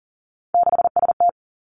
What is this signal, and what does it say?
Morse code, heard as a keyed tone, about 38 wpm, 700 Hz. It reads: 6ST
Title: Three Challenging Letter-Number Combinations 40wpm